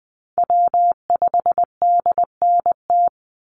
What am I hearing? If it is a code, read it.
Morse code, heard as a keyed tone, about 20 wpm, 700 Hz. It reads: W5DNT
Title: Call Signs Repeated 2x - 20wpm